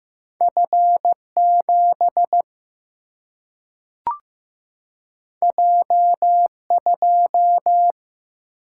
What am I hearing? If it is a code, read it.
Morse code, heard as a keyed tone, about 15 wpm, 700 Hz. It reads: F 7 J 2
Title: Single-Letter Single-Number 15wpm